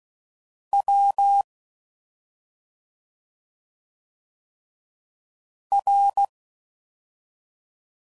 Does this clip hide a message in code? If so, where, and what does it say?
Yes: Morse code from 0.7 to 6.2 s, heard as a keyed tone, about 16 wpm, 800 Hz. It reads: W R